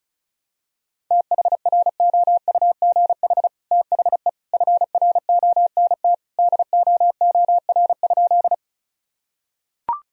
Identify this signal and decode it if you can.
Morse code, heard as a keyed tone, about 34 wpm, 700 Hz. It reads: THROUGH THE FRONT DOOR?
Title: Sentences from Top 1000 Words - Review Part 2 - 35wpm